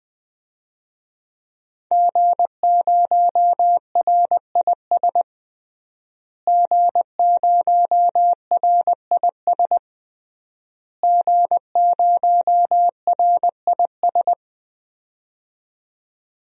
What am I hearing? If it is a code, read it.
Morse code, heard as a keyed tone, about 20 wpm, 700 Hz. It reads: G0RIS G0RIS G0RIS
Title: DXCC-100 Call Signs - 5 Characters - Repeated 3x - 20wpm